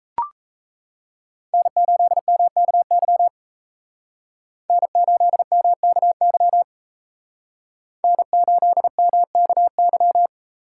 Morse code, heard as a keyed tone, about 34 wpm, 700 Hz. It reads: N8MKY N8MKY N8MKY
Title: Speed Racing - USA 1x3 Call Signs 28wpm